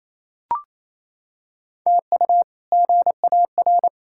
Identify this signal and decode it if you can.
Morse code, heard as a keyed tone, about 28 wpm, 700 Hz. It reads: TU GAR
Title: K1USN SST QSO Practice - 28wpm